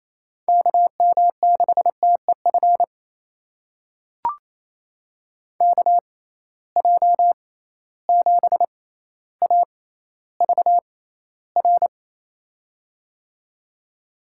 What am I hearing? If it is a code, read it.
Morse code, heard as a keyed tone, about 28 wpm, spaced as for 14 wpm, 700 Hz. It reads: KM6TEF KJ7AVR
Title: USA Call Signs - 6 Characters - 6x Character Spacing - 28wpm